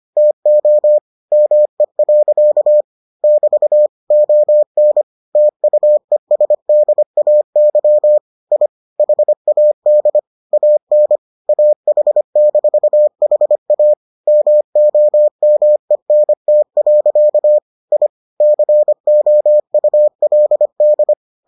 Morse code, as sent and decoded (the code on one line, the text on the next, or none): - --- / -- . .-.-.- / -...- / --- -. / - ..- . ... -.. .- -.-- / .. / .... .- -.. / .- -. / .- .... -....- .... .- / -- --- -- . -. - .-.-.- / .. / -.-. --- ..- .-.. -..
TO ME. <BT> ON TUESDAY I HAD AN AH-HA MOMENT. I COULD